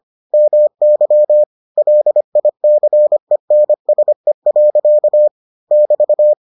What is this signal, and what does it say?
Morse code, heard as a keyed tone, about 25 wpm, 600 Hz. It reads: MY LICENSE. <BT>